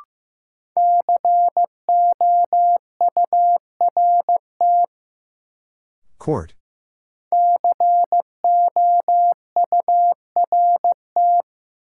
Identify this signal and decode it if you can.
Morse code, heard as a keyed tone, about 15 wpm, 700 Hz. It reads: COURT COURT
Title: Sentences from Top 1600 Words - New Word Drill Alt - 15wpm